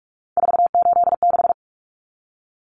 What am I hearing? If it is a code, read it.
Morse code, heard as a keyed tone, about 44 wpm, 700 Hz. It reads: 486
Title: Common Cut Numbers - 3-Digit 45wpm